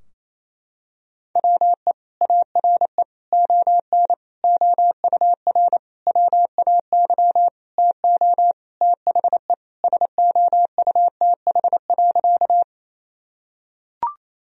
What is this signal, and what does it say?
Morse code, heard as a keyed tone, about 28 wpm, 700 Hz. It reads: WE ARE ON OUR WAY TO THE SOUTH.
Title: Sentences from Top 1000 Words - Review Part 2 - 28wpm